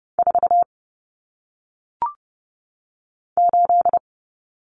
Morse code, heard as a keyed tone, about 30 wpm, 700 Hz. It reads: 4 8
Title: Numbers - 1-Digit 30wpm